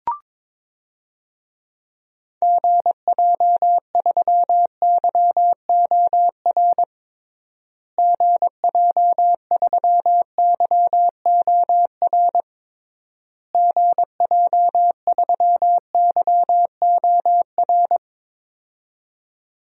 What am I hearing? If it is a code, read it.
Morse code, heard as a keyed tone, about 22 wpm, 700 Hz. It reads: GJ3YOR GJ3YOR GJ3YOR
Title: DXCC-100 Call Signs - 6 Characters - Repeated 3x - 22wpm